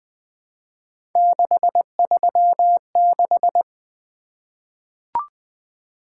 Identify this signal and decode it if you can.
Morse code, heard as a keyed tone, about 20 wpm, 700 Hz. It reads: 636
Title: Three Letter-Number Combinations - 20wpm